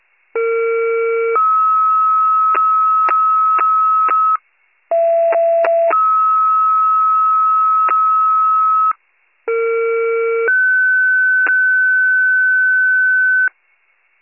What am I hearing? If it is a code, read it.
Two-tone page, heard as a one-second tone, then a three-second tone, about 460 then 1290 Hz, about 670 then 1290 Hz, about 460 then 1500 Hz, with noise. It is heard from 0.4 to 13.5 s.